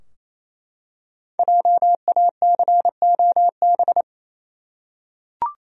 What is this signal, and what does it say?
Morse code, heard as a keyed tone, about 27 wpm, 700 Hz. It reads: JACOB
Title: USA Common Names - 28wpm